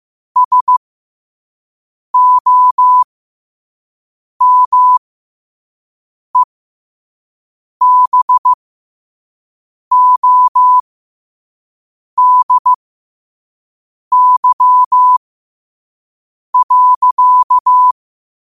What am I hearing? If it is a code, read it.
Morse code, heard as a keyed tone, about 15 wpm, spaced as for 5 wpm, 1000 Hz. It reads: SOMEBODY.
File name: qotdfiles / 5wpm